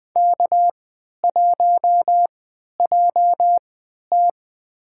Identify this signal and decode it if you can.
Morse code, heard as a keyed tone, about 20 wpm, spaced as for 11 wpm, 700 Hz. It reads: K1JT
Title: USA Call Signs - 4 Characters - 3x Character Spacing - 20wpm